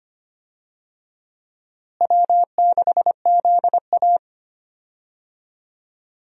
Morse code, heard as a keyed tone, about 25 wpm, 700 Hz. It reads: W6ZA
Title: USA Call Signs - 4 Characters - Repeated 2x - 25wpm